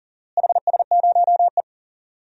Morse code, heard as a keyed tone, about 40 wpm, 700 Hz. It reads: HS0E
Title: DXCC-100 Call Signs - 4 Characters - Repeated 3x - 40wpm